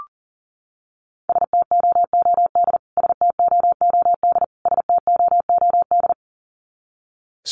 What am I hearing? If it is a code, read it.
Morse code, heard as a keyed tone, about 40 wpm, 700 Hz. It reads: STOOD STOOD STOOD
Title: Top 500 Words Repeated 40wpm